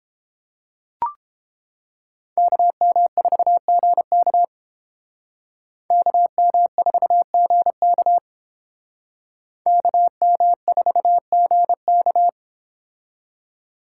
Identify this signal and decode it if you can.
Morse code, heard as a keyed tone, about 31 wpm, 700 Hz. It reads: KM4GK KM4GK KM4GK